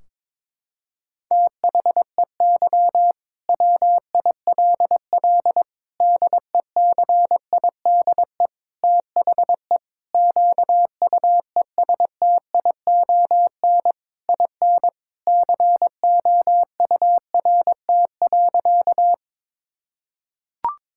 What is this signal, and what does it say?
Morse code, heard as a keyed tone, about 22 wpm, 700 Hz. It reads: THEY WILL DECIDE THE QUESTION IN COURT.
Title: Sentences from Top 1600 Words 22wpm